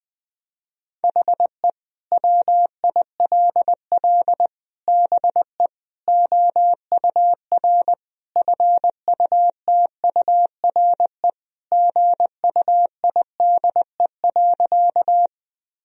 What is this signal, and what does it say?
Morse code, heard as a keyed tone, about 20 wpm, 700 Hz. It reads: HE WILL BE OUR FUTURE GUIDE.
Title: Sentences from Top 1000 Words - Review Part 3 - 20wpm